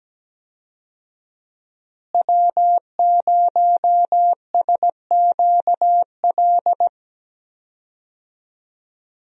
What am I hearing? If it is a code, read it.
Morse code, heard as a keyed tone, about 17 wpm, 700 Hz. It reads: W0SQL
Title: USA Call Signs - 5 Characters - Repeated 2x - 17wpm